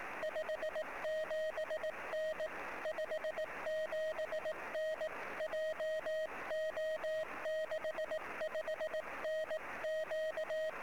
Morse code, heard as a keyed tone, about 18 wpm, 620 Hz, with noise. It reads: 57N 57N JO65NQ
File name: sm7lcb / 11 sm7ecm 23cm